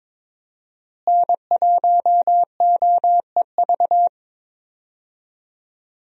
Morse code, heard as a keyed tone, about 22 wpm, 700 Hz. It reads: N1OEV
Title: USA Call Signs - 5 Characters - Repeated 3x - 22wpm